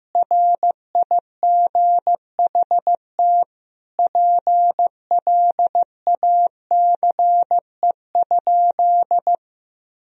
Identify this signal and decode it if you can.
Morse code, heard as a keyed tone, about 15 wpm, 700 Hz. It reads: RIGHT PLACE?
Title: Sentences from Top 1000 Words - Review Part 2 - 15wpm